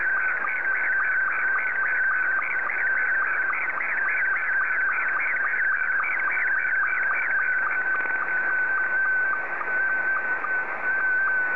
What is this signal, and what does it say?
SSTV (slow-scan television), Scottie 2.
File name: sstv / S2-JA7QJQ